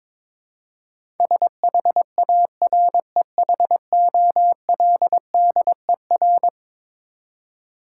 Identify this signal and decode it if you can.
Morse code, heard as a keyed tone, about 22 wpm, 700 Hz. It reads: SHAREHOLDER
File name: practice / 22wpm / Eleven Letter Words - 22wpm